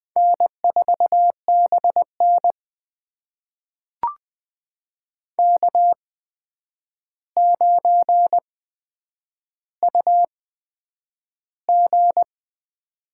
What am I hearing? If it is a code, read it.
Morse code, heard as a keyed tone, about 20 wpm, spaced as for 9 wpm, 700 Hz. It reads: N4BN K9UG
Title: USA Call Signs - 4 Characters - 8x Character Spacing - 20wpm